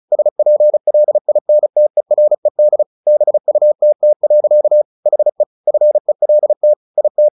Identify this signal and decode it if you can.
Morse code, heard as a keyed tone, about 34 wpm, 600 Hz. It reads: SPLINTERED BUTT. HE FELT IT
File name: the old man and the sea 2 35wpm